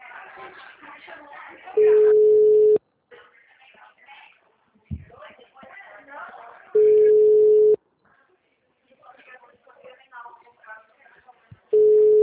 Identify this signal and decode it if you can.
Ringback tone (Europe, 425 Hz), with speech.